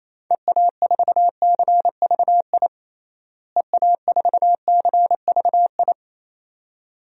Morse code, heard as a keyed tone, about 28 wpm, 700 Hz. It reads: EA4CVI EA4CVI
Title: DXCC-100 Call Signs - 6 Characters - Repeated 2x - 28wpm